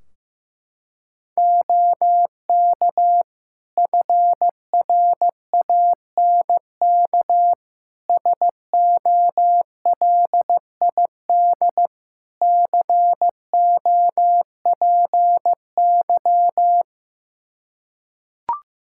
Morse code, heard as a keyed tone, about 16 wpm, 700 Hz. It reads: OK FRANK SOLID COPY
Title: Practice QSO - Exchange 3 - More Info - 15wpm